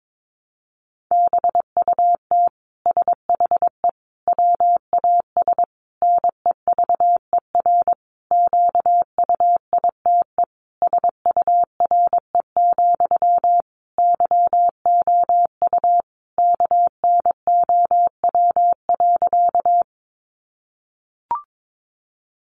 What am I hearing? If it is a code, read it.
Morse code, heard as a keyed tone, about 22 wpm, 700 Hz. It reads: BUT SHE WAS NEVER QUITE SURE, YOU KNOW.
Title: Sentences from Top 700 Words 22wpm